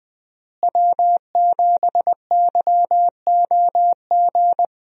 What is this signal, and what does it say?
Morse code, heard as a keyed tone, about 20 wpm, 700 Hz. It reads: W7YOG
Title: USA Call Signs - 5 Characters - Repeated 3x - 20wpm